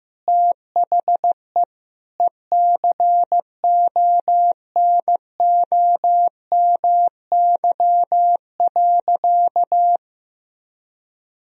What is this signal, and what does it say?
Morse code, heard as a keyed tone, about 15 wpm, 700 Hz. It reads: THE ECONOMY.
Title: Sentences from Top 1000 Words - Review Part 3 - 15wpm